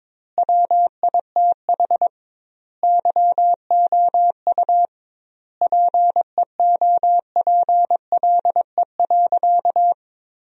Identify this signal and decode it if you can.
Morse code, heard as a keyed tone, about 22 wpm, 700 Hz. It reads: WITH YOU PEOPLE.
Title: 2x Word-Spacing - Sentences from Top 300 Words - 22wpm